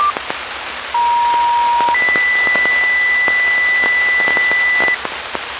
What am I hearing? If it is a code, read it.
Two-tone page, heard as a one-second tone, then a three-second tone, about 950 then 2000 Hz, with noise. It is heard from 0.9 to 4.9 s.